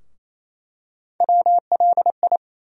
Morse code, heard as a keyed tone, about 27 wpm, 700 Hz. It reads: WLI
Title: USA Call Sign Postfixes - 3 Letters 28wpm